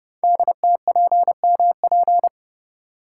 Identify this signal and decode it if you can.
Morse code, heard as a keyed tone, about 30 wpm, 700 Hz. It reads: DTPMP
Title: Code Groups - 5 Letters - 30wpm